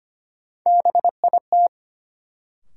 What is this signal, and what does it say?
Morse code, heard as a keyed tone, about 25 wpm, 700 Hz. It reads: BIT
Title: Sets of 3 Words - New Word Drill - 25wpm